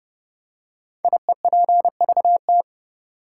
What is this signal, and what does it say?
Morse code, heard as a keyed tone, about 30 wpm, 700 Hz. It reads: IEPVT
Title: Code Groups - 5 Letters - 30wpm